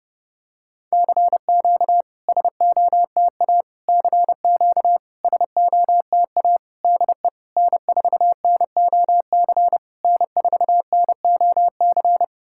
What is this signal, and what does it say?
Morse code, heard as a keyed tone, about 30 wpm, 700 Hz. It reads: CQ SOTA CQ SOTA DE N4NOC N4NOC
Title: SOTA QSO Practice 30wpm